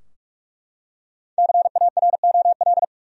Morse code, heard as a keyed tone, about 41 wpm, 700 Hz. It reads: CAROL
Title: USA Common Names - Max 5 Letters - 45wpm